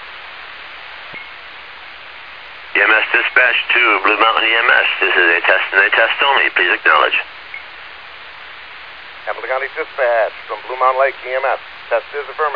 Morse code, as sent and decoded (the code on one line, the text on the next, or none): none